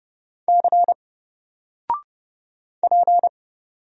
Morse code, heard as a keyed tone, about 31 wpm, 700 Hz. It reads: C P
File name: Single-Letter-Number-30wpm